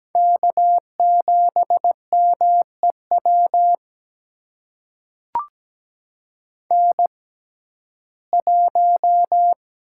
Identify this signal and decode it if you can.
Morse code, heard as a keyed tone, about 17 wpm, spaced as for 12 wpm, 700 Hz. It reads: K7MEW N1
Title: USA Call Signs - 5 Characters - 6x Character Spacing - 17wpm